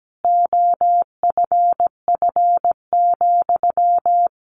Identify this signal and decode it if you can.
Morse code, heard as a keyed tone, about 17 wpm, 700 Hz. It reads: OFF,
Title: A Sound Of Thunder 17wpm